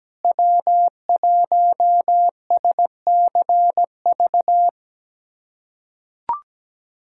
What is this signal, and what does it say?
Morse code, heard as a keyed tone, about 17 wpm, 700 Hz. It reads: W1SCV
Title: USA Callsign DE Callsign - 5 Characters - 17wpm